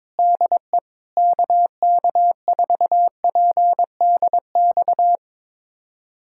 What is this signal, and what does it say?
Morse code, heard as a keyed tone, about 22 wpm, 700 Hz. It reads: DE KK4PDX
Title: USA Callsign DE Callsign - 6 Characters - 22wpm